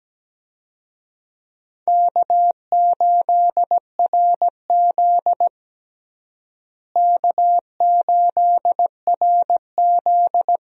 Morse code, heard as a keyed tone, about 17 wpm, 700 Hz. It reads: K8RZ K8RZ
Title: USA Call Signs - 4 Characters - Repeated 2x - 17wpm